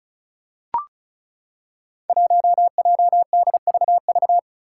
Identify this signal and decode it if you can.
Morse code, heard as a keyed tone, about 36 wpm, 700 Hz. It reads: 1JDVV
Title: Five Challenging Letter-Number Combinations 35wpm